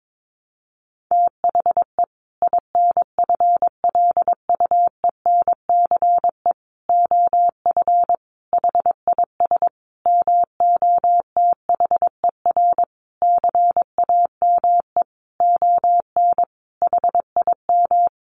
Morse code, heard as a keyed tone, about 22 wpm, 700 Hz. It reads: THE INFLUENCE OF HIS MOTHER CAME ON HIM
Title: Sentences from Top 1100 Words 22wpm